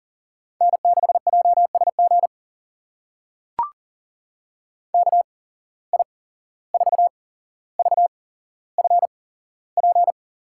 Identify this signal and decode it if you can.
Morse code, heard as a keyed tone, about 40 wpm, spaced as for 16 wpm, 700 Hz. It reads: N6JSG KI4VFP
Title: USA Call Signs - 8x Character Spacing - 40wpm